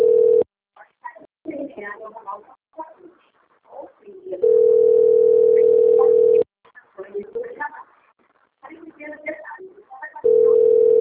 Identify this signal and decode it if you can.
Ringback tone (North America), 440 + 480 Hz, with speech.
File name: rg-1600-unknown- - - .1255